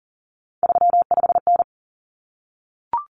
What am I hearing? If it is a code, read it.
Morse code, heard as a keyed tone, about 40 wpm, 700 Hz. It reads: 35N
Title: Common Cut Numbers - 3-Digit 40wpm